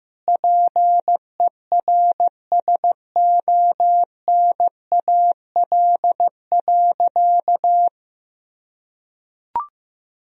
Morse code, heard as a keyed tone, about 15 wpm, 700 Hz. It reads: PERSONAL.
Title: Sentences from Top 1000 Words - Review Part 4 - 15wpm